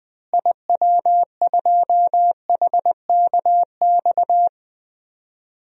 Morse code, heard as a keyed tone, about 20 wpm, 700 Hz. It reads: IW2HKX
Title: DXCC-100 Call Signs - 6 Characters - Repeated 3x - 20wpm